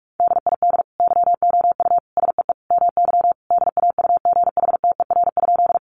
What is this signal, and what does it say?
Morse code, heard as a keyed tone, about 44 wpm, 700 Hz. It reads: DID YOU SEE MY DAUGHTER?
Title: Sentences from Top 1500 Words 45wpm